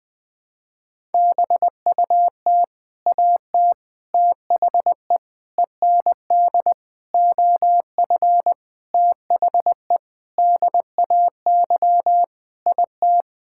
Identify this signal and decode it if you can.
Morse code, heard as a keyed tone, about 20 wpm, 700 Hz. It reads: BUT AT THE END OF THE DAY IT
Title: Sentences with 8-Word Phrases - 20wpm